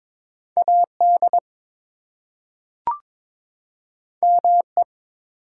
Morse code, heard as a keyed tone, about 22 wpm, 700 Hz. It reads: AD ME
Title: Two Letter Words - 22wpm